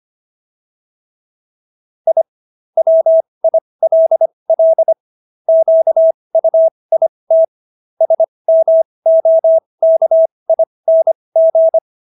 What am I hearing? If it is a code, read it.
Morse code, heard as a keyed tone, about 25 wpm, spaced as for 20 wpm, 650 Hz. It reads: I WILL QUIT SMOKING